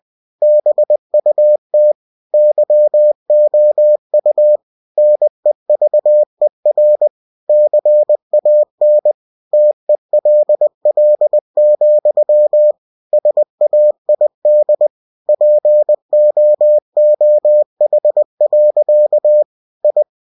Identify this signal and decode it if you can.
Morse code, heard as a keyed tone, about 20 wpm, 600 Hz. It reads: BUT YOU NEVER CAN TELL, SAID POOH. I
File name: cw-books / winnie-the-pooh / winnie-20wpm-page-0037